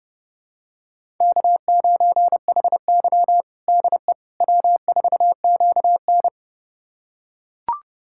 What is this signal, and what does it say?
Morse code, heard as a keyed tone, about 32 wpm, 700 Hz. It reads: K9HY DE W4QN
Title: USA Callsign DE Callsign - 4 Characters - 30wpm